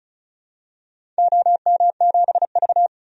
Morse code, heard as a keyed tone, about 36 wpm, 700 Hz. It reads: OM7V
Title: DXCC-100 Call Signs - 4 Characters - 35wpm